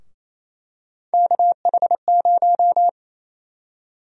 Morse code, heard as a keyed tone, about 28 wpm, 700 Hz. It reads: KH0
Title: USA Call Sign Prefixes - 28wpm